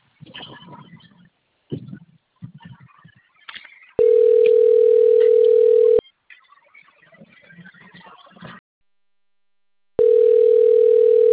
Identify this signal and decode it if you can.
Ringback tone (North America), 440 + 480 Hz.
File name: rg-1600-unknown- - - .68135